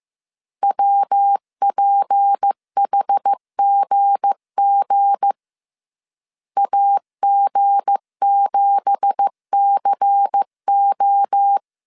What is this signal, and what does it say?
Morse code, heard as a keyed tone, about 15 wpm, 800 Hz. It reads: WPHGG AG7CO
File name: CW / DET / 1 9 1500C S C 14wpm 3245